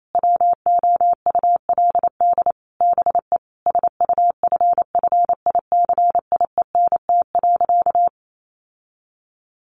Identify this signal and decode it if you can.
Morse code, heard as a keyed tone, about 28 wpm, 700 Hz. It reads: WOULD BE SUFFICIENT.
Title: Sentences from Top 1400 Words 28wpm